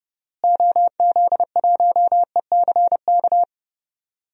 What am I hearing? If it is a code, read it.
Morse code, heard as a keyed tone, about 32 wpm, 700 Hz. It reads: OZ1ECK
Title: DXCC-100 Call Signs - 6 Characters - Repeated 3x - 30wpm